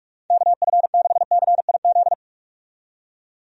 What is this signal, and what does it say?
Morse code, heard as a keyed tone, about 43 wpm, 700 Hz. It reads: KF6CIZ
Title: USA Call Signs - 6 Characters - 45wpm